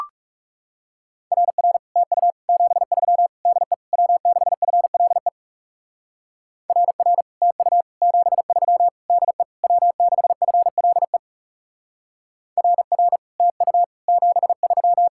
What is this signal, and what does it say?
Morse code, heard as a keyed tone, about 40 wpm, 700 Hz. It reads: RR TU 73 DE W6FLE RR TU 73 DE W6FLE RR TU 73